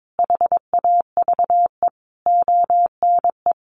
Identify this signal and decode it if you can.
Morse code, heard as a keyed tone, about 22 wpm, 700 Hz. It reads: HAVE ONE
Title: Sentences from Top 1000 Words 22wpm